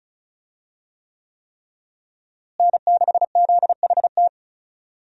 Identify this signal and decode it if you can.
Morse code, heard as a keyed tone, about 36 wpm, 700 Hz. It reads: N6ZHT
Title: USA Call Signs - 5 Characters - Repeated 3x - 35wpm